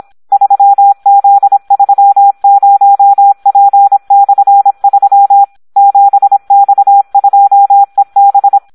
Morse code, heard as a keyed tone, about 26 wpm, 800 Hz. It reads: 3Z30P/3 7X2EB
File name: Audio practica CW / SD 1 26